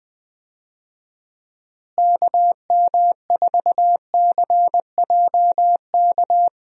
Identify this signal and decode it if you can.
Morse code, heard as a keyed tone, about 21 wpm, 700 Hz. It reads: KM4CJK
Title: USA Call Signs - 6 Characters - Repeated 3x - 20wpm